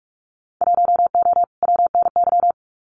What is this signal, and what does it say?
Morse code, heard as a keyed tone, about 46 wpm, 700 Hz. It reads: 1O WNY
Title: Winter Field Day Exchanges 45wpm